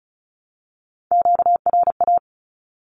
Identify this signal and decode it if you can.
Morse code, heard as a keyed tone, about 34 wpm, 700 Hz. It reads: QRA
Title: Q-Codes 35wpm